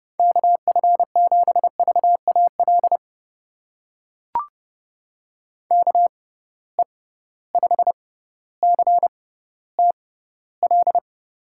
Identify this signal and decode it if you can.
Morse code, heard as a keyed tone, about 29 wpm, spaced as for 15 wpm, 700 Hz. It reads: KF7VAL KE5CTL
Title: USA Call Signs - 6 Characters - 6x Character Spacing - 30wpm